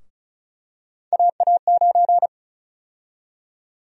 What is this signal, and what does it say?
Morse code, heard as a keyed tone, about 36 wpm, 700 Hz. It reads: AA9
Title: USA Call Sign Prefixes - 35wpm